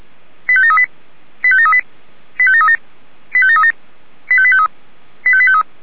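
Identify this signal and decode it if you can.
Five-tone selective call (ZVEI), heard as quick runs of five tones, tones of about 70 ms. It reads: 85738 85738 85737 85737 85853 85853
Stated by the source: ZVEI five-tone call received by SDR.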